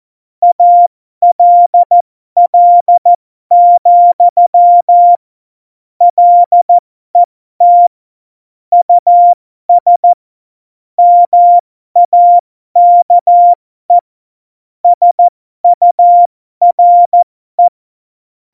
Morse code, heard as a keyed tone, about 14 wpm, spaced as for 12 wpm, 700 Hz. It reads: ALL, LET US MAKE SURE